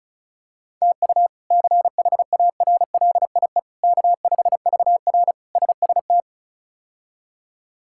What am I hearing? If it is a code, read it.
Morse code, heard as a keyed tone, about 36 wpm, 700 Hz. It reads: TU CHARLIE K5VR SST
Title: K1USN SST QSO Practice - 35wpm